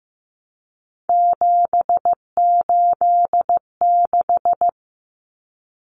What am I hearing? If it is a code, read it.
Morse code, heard as a keyed tone, about 15 wpm, 700 Hz. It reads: 786
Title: Common Cut Numbers - 3-Digit 15wpm